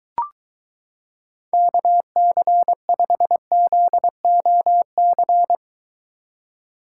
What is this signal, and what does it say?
Morse code, heard as a keyed tone, about 23 wpm, 700 Hz. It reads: KC5ZOC